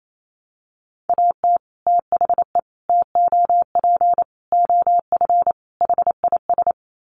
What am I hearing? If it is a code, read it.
Morse code, heard as a keyed tone, about 28 wpm, 700 Hz. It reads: AT THE TOP OF HIS
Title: Sets of 5 Words Encore 28wpm